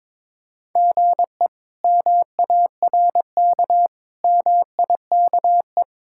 Morse code, heard as a keyed tone, about 22 wpm, 700 Hz. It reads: GE MARK MIKE
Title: K1USN SST QSO Practice - 22wpm